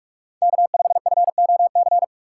Morse code, heard as a keyed tone, about 43 wpm, 700 Hz. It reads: K5FOC
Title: USA Call Signs - 5 Characters - 2x Character Spacing - 45wpm